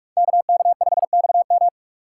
Morse code, heard as a keyed tone, about 48 wpm, 700 Hz. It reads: KK5XM
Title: Speed Racing - USA 2x2 Call Signs 45wpm